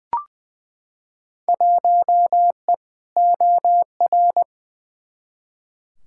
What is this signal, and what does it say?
Morse code, heard as a keyed tone, about 20 wpm, 700 Hz. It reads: 1E OR
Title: Field Day Exchanges Encore - 20wpm